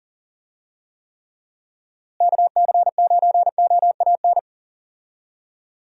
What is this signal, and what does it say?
Morse code, heard as a keyed tone, about 40 wpm, 700 Hz. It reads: KC9OAN
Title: USA Call Signs - 6 Characters - Repeated 3x - 40wpm